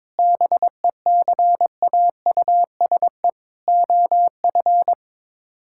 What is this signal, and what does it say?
Morse code, heard as a keyed tone, about 22 wpm, 700 Hz. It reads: BECAUSE OF
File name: Sets of 9 Words - 22wpm